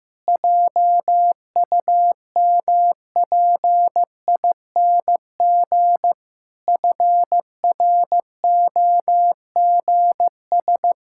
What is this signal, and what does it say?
Morse code, heard as a keyed tone, about 15 wpm, 700 Hz. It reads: JUMPING FROGS
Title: Pangram Challenge v2 - 15wpm